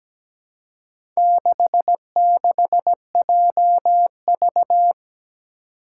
Morse code, heard as a keyed tone, about 17 wpm, 700 Hz. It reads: 66JV
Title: Four Challenging Letter-Number Combinations 17wpm